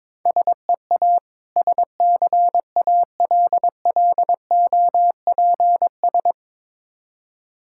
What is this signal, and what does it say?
Morse code, heard as a keyed tone, about 22 wpm, 690 Hz. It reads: SEA SCALLOPS
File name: Difficult Words with Hints - 22wpm